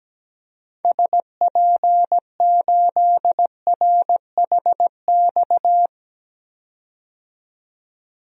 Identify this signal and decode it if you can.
Morse code, heard as a keyed tone, about 17 wpm, 700 Hz. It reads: SP8RHX
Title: DXCC-100 Call Signs - 6 Characters - 17wpm